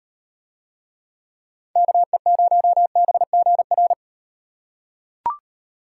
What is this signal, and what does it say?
Morse code, heard as a keyed tone, about 38 wpm, 700 Hz. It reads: KE0BGR